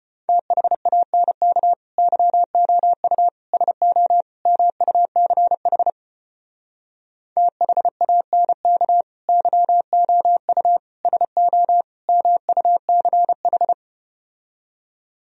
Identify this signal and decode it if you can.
Morse code, heard as a keyed tone, about 33 wpm, 700 Hz. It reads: THANK YOU SO MUCH THANK YOU SO MUCH